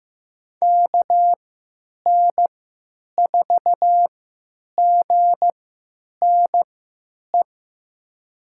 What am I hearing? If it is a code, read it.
Morse code, heard as a keyed tone, about 15 wpm, spaced as for 9 wpm, 700 Hz. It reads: KN4GNE
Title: USA Call Signs - 6 Characters - 3x Character Spacing - 15wpm